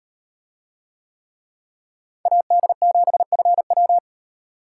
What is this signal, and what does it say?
Morse code, heard as a keyed tone, about 36 wpm, 700 Hz. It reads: AD7FW